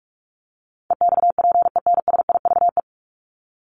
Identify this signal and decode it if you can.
Morse code, heard as a keyed tone, about 44 wpm, 700 Hz. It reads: EXPENSIVE
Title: Top 751-1000 Words 45wpm